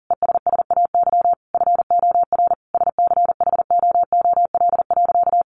Morse code, heard as a keyed tone, about 40 wpm, 700 Hz. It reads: ESSAY FOR SCHOOL.
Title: Sentences from Top 1500 Words 40wpm